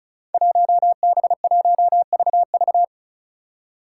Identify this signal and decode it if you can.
Morse code, heard as a keyed tone, about 36 wpm, 700 Hz. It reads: 1B1VV
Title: Five Challenging Letter-Number Combinations 35wpm